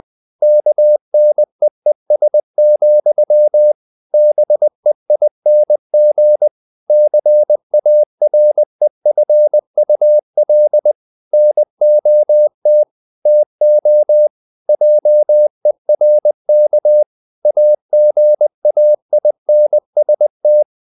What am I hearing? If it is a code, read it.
Morse code, heard as a keyed tone, about 20 wpm, 600 Hz. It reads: KNEES, BEING CAREFUL NOT TO JERK AGAINST
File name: the old man and the sea part 2 20wpm